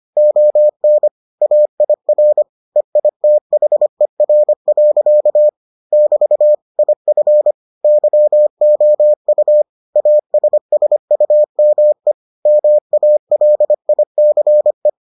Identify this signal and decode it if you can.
Morse code, heard as a keyed tone, about 25 wpm, 600 Hz. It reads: ON AIR EITHER. <BT> IF YOU ASSUME MALICE